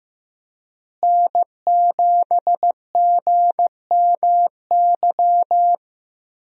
Morse code, heard as a keyed tone, about 15 wpm, 700 Hz. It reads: N7GMY
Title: USA Call Signs - 5 Characters - Repeated 3x - 15wpm